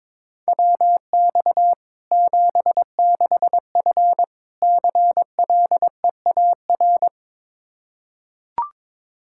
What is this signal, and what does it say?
Morse code, heard as a keyed tone, about 22 wpm, 700 Hz. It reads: WX 76F CLEAR
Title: K1USN SST QSO Practice 22wpm